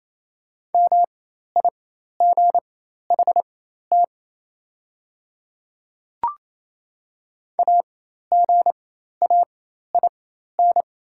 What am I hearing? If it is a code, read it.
Morse code, heard as a keyed tone, about 28 wpm, spaced as for 13 wpm, 700 Hz. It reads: MIGHT AGAIN
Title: 4x Character Spacing - Top 500 Words - 28wpm